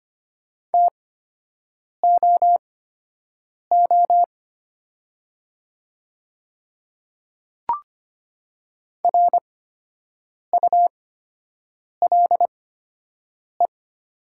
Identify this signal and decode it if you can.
Morse code, heard as a keyed tone, about 25 wpm, spaced as for 7 wpm, 700 Hz. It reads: TOO RULE
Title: 8x Character Spacing - Top 500 Words - 25wpm